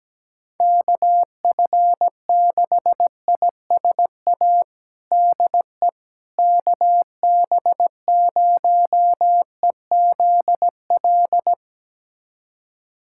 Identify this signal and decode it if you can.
Morse code, heard as a keyed tone, about 17 wpm, 700 Hz. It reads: KF6ISA DE KB0EZL
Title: USA Callsign DE Callsign - 6 Characters - 17wpm